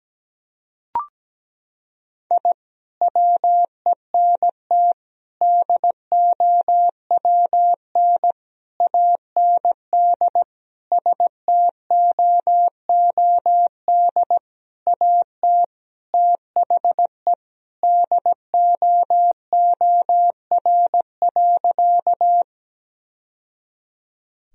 Morse code, heard as a keyed tone, about 17 wpm, 700 Hz. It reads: I WENT DOWN AND STOOD AT THE DOOR.
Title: Sentences from Top 1000 Words - Review Part 2 - 17wpm